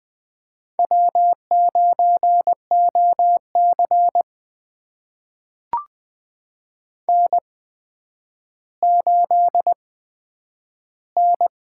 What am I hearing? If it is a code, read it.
Morse code, heard as a keyed tone, about 20 wpm, spaced as for 10 wpm, 700 Hz. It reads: W9OC N8N
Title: USA Call Signs - 8x Character Spacing - 20wpm